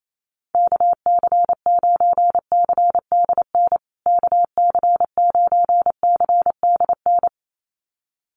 Morse code, heard as a keyed tone, about 28 wpm, 700 Hz. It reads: KC9CDN KC9CDN
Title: USA Call Signs Repeated 2x 28wpm